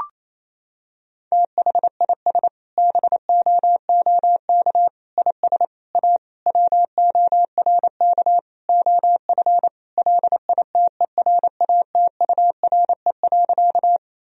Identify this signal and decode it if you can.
Morse code, heard as a keyed tone, about 28 wpm, 700 Hz. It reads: THIS BOOK IS A WORK OF LITERATURE.
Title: Sentences from Top 1000 Words - Review Part 3 - 28wpm